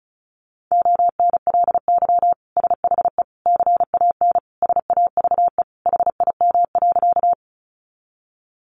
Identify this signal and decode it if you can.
Morse code, heard as a keyed tone, about 34 wpm, 700 Hz. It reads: ONLY SHE CAN SAVE HIM.
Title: Sentences from Top 1100 Words 35wpm